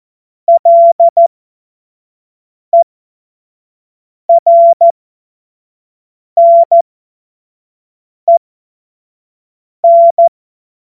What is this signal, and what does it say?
Morse code, heard as a keyed tone, about 14 wpm, spaced as for 5 wpm, 680 Hz. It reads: LERNEN
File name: Morsekurs / Witze / Witz 03 Juni 2025 25 F70